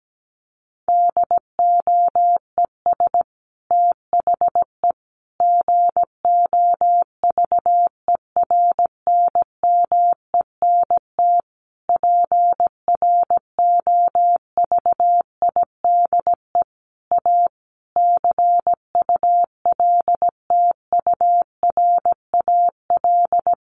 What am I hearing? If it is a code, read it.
Morse code, heard as a keyed tone, about 17 wpm, 700 Hz. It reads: DOES THE GOVERNMENT PROVIDE A CULTURAL
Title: Sentences from Top 1500 Words 17wpm